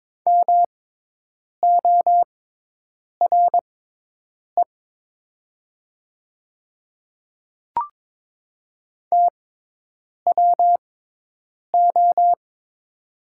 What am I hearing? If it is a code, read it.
Morse code, heard as a keyed tone, about 22 wpm, spaced as for 8 wpm, 700 Hz. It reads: MORE TWO
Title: 6x Character Spacing - Top 100 Words - 22wpm